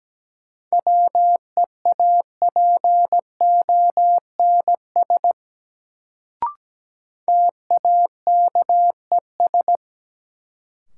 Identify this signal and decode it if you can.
Morse code, heard as a keyed tone, about 17 wpm, 700 Hz. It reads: WEAPONS TAKES
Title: Sets of 5 Words Encore - New Word Drill - 17wpm